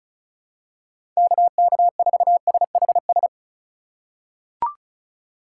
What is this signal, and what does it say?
Morse code, heard as a keyed tone, about 34 wpm, 700 Hz. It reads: KK4SHS
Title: USA Call Signs - 6 Characters - 2x Character Spacing - 35wpm